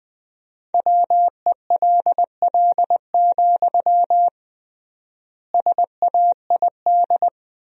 Morse code, heard as a keyed tone, about 21 wpm, 700 Hz. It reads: WELL, SAID
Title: 3x Word-Spacing - Sentences from Top 400 Words - 20wpm